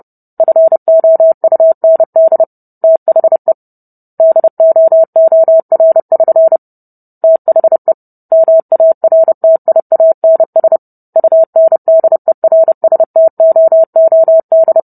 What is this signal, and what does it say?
Morse code, heard as a keyed tone, about 30 wpm, 650 Hz. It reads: FOUND THE DOOR<SN> THE MARTIANS UNDERSTOOD